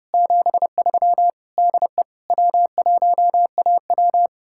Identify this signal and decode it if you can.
Morse code, heard as a keyed tone, about 38 wpm, 700 Hz. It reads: 73 DE W1AW